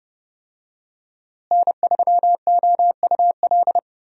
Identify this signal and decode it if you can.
Morse code, heard as a keyed tone, about 30 wpm, 700 Hz. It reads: N3OUL
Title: USA Call Signs - 5 Characters - Repeated 3x - 30wpm